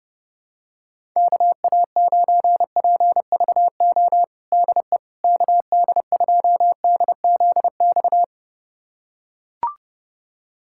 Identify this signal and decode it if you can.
Morse code, heard as a keyed tone, about 31 wpm, 700 Hz. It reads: KA9PVO DE KD2DZX
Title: USA Callsign DE Callsign - 6 Characters - 30wpm